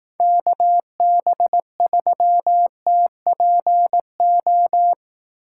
Morse code, heard as a keyed tone, about 18 wpm, 700 Hz. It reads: KB3TPO
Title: Speed Racing - USA 2x3 Call Signs 15wpm